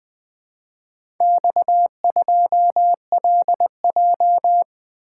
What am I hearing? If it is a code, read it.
Morse code, heard as a keyed tone, about 20 wpm, 700 Hz. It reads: X2LJ
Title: Four Letter-Number Combinations - 20wpm